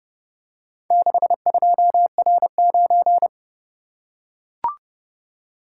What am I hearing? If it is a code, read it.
Morse code, heard as a keyed tone, about 30 wpm, 700 Hz. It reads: 62R9